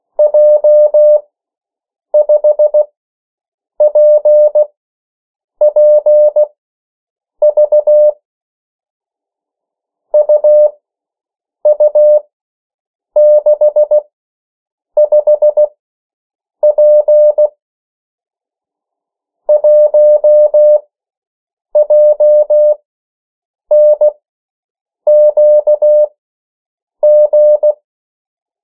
Morse code, heard as a keyed tone, about 16 wpm, spaced as for 7 wpm, 600 Hz. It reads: J5PPV UU65P 1JNQG
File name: morse mp3 / 14 urok 13